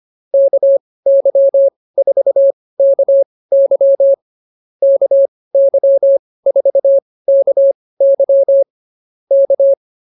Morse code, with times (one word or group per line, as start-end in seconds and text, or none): 0.3-4.1 KY4KY
4.8-8.6 KY4KY
9.3-9.7 K